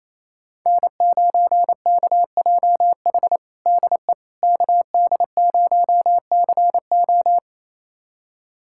Morse code, heard as a keyed tone, about 28 wpm, 700 Hz. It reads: N9KJH DE KD0CO
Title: USA Callsign DE Callsign - 5 Characters - 28wpm